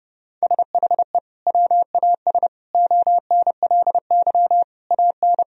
Morse code, heard as a keyed tone, about 30 wpm, 700 Hz. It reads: SHE WAS ONLY AN
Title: Sentences from Top 1000 Words - Review Part 4 - 30wpm